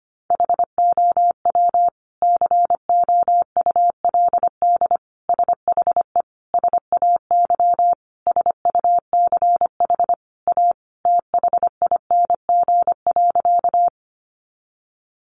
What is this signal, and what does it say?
Morse code, heard as a keyed tone, about 25 wpm, 700 Hz. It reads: HOW COULD SHE SAY SUCH A THING.
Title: Sentences from Top 200 Words 25wpm